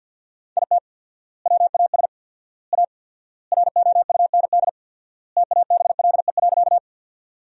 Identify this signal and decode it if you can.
Morse code, heard as a keyed tone, about 50 wpm, 700 Hz. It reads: IT WAS A ROUND TABLE.
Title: 4x Word-Spacing - Sentences from Top 400 Words - 50wpm